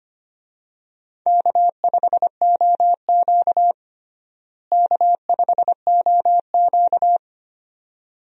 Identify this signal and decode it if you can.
Morse code, heard as a keyed tone, about 25 wpm, 700 Hz. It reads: K5OQ K5OQ
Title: USA Call Signs - 4 Characters - Repeated 2x - 25wpm